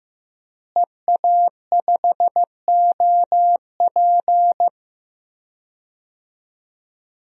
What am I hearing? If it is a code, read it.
Morse code, heard as a keyed tone, about 15 wpm, 700 Hz. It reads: EA5OP
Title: DXCC-100 Call Signs - 5 Characters - Repeated 2x - 15wpm